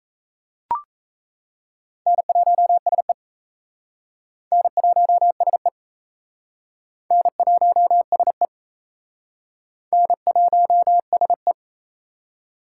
Morse code, heard as a keyed tone, about 37 wpm, 700 Hz. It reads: N1SE N1SE N1SE N1SE